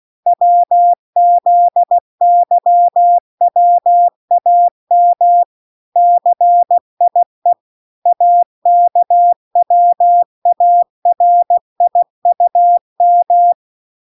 Morse code, heard as a keyed tone, about 16 wpm, 700 Hz. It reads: WZYWAM CIE AKWARIUM